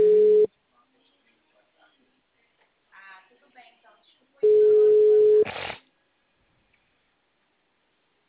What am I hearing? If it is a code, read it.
Ringback tone (Europe, 425 Hz), with speech.
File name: out- -2001- - - .70869